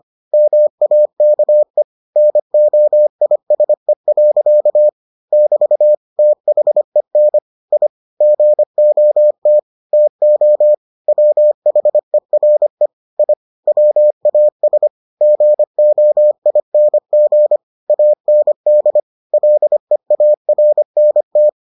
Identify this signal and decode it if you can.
Morse code, heard as a keyed tone, about 25 wpm, 600 Hz. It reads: MAKE NOISE. <BT> THEN I GOT TO WHERE I WAS GOING AND LEARNT